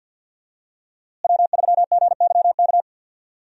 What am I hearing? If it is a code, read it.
Morse code, heard as a keyed tone, about 53 wpm, 700 Hz. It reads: W3GYK
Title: USA Call Signs - 5 Characters - Repeated 3x - 50wpm